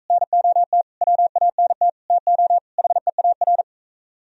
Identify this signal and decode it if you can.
Morse code, heard as a keyed tone, about 41 wpm, 700 Hz. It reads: NOT WANT TO HEAR